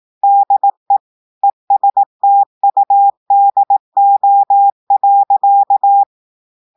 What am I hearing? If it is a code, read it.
Morse code, heard as a keyed tone, about 18 wpm, 800 Hz. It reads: DE ESTUDO.
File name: audios / portugais / portugais texte en clair 1 vitesse 18 mots minutes